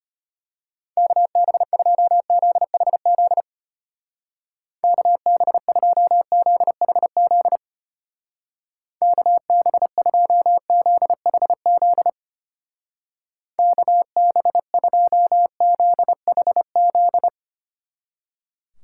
Morse code, heard as a keyed tone, about 33 wpm, 700 Hz. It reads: KB2ZHZ KB2ZHZ KB2ZHZ KB2ZHZ